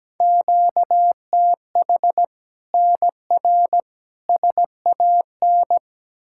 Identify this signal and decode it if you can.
Morse code, heard as a keyed tone, about 17 wpm, 700 Hz. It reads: QTH NR SAN
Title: Practice QSO - Exchange 1 - Beginning - 17wpm